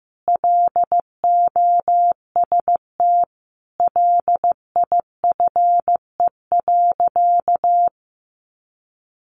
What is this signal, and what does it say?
Morse code, heard as a keyed tone, about 14 wpm, 700 Hz. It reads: LOST LIFE.
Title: Sentences from Top 1400 Words 15wpm